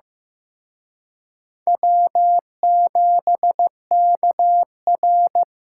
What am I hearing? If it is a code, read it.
Morse code, heard as a keyed tone, about 18 wpm, 700 Hz. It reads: W7KR
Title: Speed Racing - USA 1x2 Call Signs 15wpm